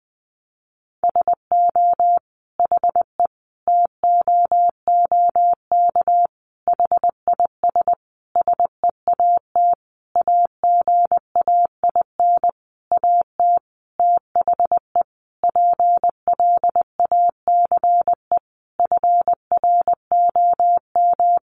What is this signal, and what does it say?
Morse code, heard as a keyed tone, about 20 wpm, 700 Hz. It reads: SO HE TOOK HIS SEAT AGAIN AT THE PLACE FROM
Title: Sentences from Top 1500 Words 20wpm